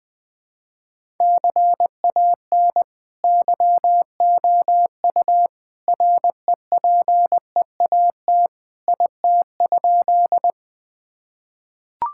Morse code, heard as a keyed tone, about 20 wpm, 700 Hz. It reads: CAN YOU REPEAT IT?
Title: Sentences from Top 1600 Words 20wpm